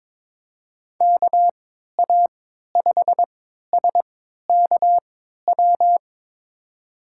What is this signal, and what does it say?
Morse code, heard as a keyed tone, about 22 wpm, spaced as for 12 wpm, 700 Hz. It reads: KA5SKW
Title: USA Call Signs - 6 Characters - 3x Character Spacing - 22wpm